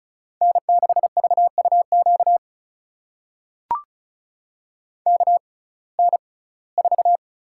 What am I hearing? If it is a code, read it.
Morse code, heard as a keyed tone, about 36 wpm, spaced as for 21 wpm, 700 Hz. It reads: N6VUQ KN4
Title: USA Call Signs - 5 Characters - 6x Character Spacing - 35wpm